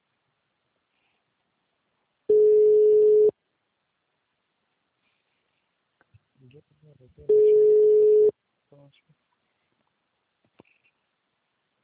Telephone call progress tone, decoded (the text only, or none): ringback tone